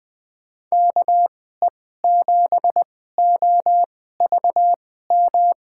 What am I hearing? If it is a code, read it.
Morse code, heard as a keyed tone, about 20 wpm, spaced as for 15 wpm, 700 Hz. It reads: KE7OVM
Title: USA Call Signs - 6 Characters - 2x Character Spacing - 20wpm